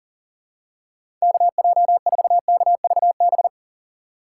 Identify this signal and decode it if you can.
Morse code, heard as a keyed tone, about 40 wpm, 700 Hz. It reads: KJ4KVB
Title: USA Call Signs - 6 Characters - 3x Character Spacing - 40wpm